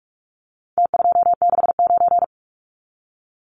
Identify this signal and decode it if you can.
Morse code, heard as a keyed tone, about 44 wpm, 700 Hz. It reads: T269
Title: Numbers - Contest Style - 4-Digit 45wpm